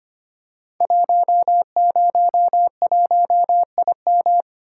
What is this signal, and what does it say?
Morse code, heard as a keyed tone, about 25 wpm, 700 Hz. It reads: 101IM
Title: Code Groups - 5 Letter:Numbers - 25wpm